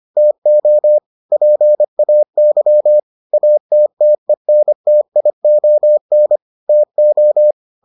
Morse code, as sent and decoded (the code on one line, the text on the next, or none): - --- / .--. .- -.-- / .- - - . -. - .. --- -. / - ---
TO PAY ATTENTION TO